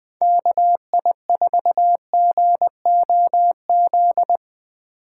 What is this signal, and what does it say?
Morse code, heard as a keyed tone, about 21 wpm, 700 Hz. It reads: KI4GOZ